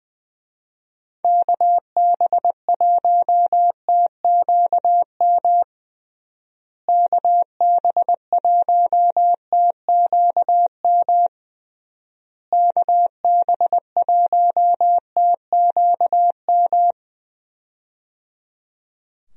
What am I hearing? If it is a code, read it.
Morse code, heard as a keyed tone, about 20 wpm, 700 Hz. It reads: KB1TQM KB1TQM KB1TQM
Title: USA Call Signs - 6 Characters - Repeated 3x - 20wpm